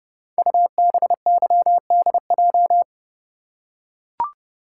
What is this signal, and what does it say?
Morse code, heard as a keyed tone, about 28 wpm, 700 Hz. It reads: UBYDJ
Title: Code Groups - 5 Letters - 30wpm